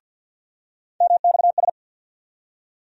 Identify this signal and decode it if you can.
Morse code, heard as a keyed tone, about 48 wpm, 700 Hz. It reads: MXS
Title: USA Call Sign Postfixes - 3 Letters 50wpm